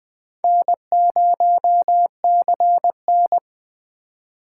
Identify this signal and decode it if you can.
Morse code, heard as a keyed tone, about 21 wpm, 700 Hz. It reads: N0CN